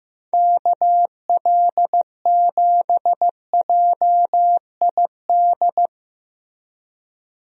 Morse code, heard as a keyed tone, about 16 wpm, 700 Hz. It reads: KL7JID
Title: DXCC-100 Call Signs - 6 Characters - Repeated 2x - 15wpm